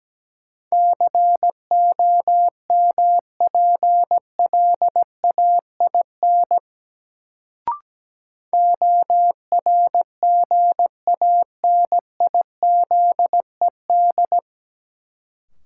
Morse code, heard as a keyed tone, about 17 wpm, 700 Hz. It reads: COMPLAIN ORGANIZED
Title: Sentences from Top 1300 Words - New Word Drill Alt - 17wpm